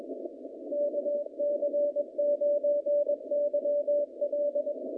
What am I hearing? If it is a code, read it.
Morse code, heard as a keyed tone, about 21 wpm, 580 Hz. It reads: KC9YL
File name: contests / naqp-cw-k5tr-op-k5pi- -0300